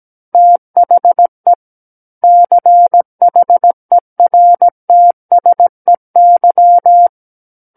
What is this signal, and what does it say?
Morse code, heard as a keyed tone, about 17 wpm, 700 Hz. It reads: THE CHERTSEY
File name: Book One / 08-Chapter 8 - 17 WPM